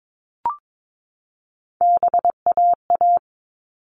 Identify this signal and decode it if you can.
Morse code, heard as a keyed tone, about 22 wpm, 700 Hz. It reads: BAA
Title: All Cut Numbers - 3-Digit 22wpm